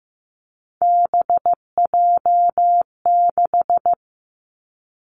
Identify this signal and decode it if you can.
Morse code, heard as a keyed tone, about 15 wpm, 700 Hz. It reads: BJ6
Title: Three Challenging Letter-Number Combinations 15wpm